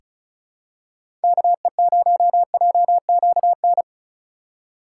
Morse code, heard as a keyed tone, about 36 wpm, 700 Hz. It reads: KE0JQN
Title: USA Call Signs - 6 Characters - 4x Character Spacing - 35wpm